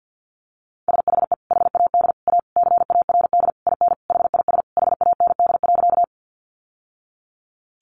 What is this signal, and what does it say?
Morse code, heard as a keyed tone, about 50 wpm, 700 Hz. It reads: SHE HAD A CARD IN HIS HAND.
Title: Sentences from Top 600 Words 50wpm